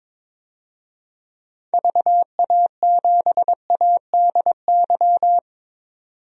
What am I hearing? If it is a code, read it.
Morse code, heard as a keyed tone, about 21 wpm, 700 Hz. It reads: VA7ADY
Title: DXCC-100 Call Signs - 6 Characters - 22wpm